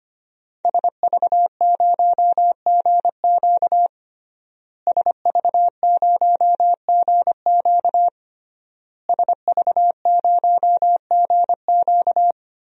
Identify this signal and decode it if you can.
Morse code, heard as a keyed tone, about 25 wpm, 700 Hz. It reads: SV0GQ SV0GQ SV0GQ
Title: DXCC-100 Call Signs - 5 Characters - Repeated 3x - 25wpm